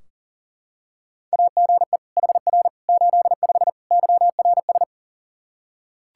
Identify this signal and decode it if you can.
Morse code, heard as a keyed tone, about 40 wpm, 700 Hz. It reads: AGE HR 85 YRS
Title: Practice QSO - Exchange 3 - More Info - 40wpm